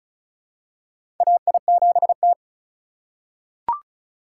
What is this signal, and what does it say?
Morse code, heard as a keyed tone, about 36 wpm, 700 Hz. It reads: AI7T
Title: USA Call Signs - 4x Character Spacing - 35wpm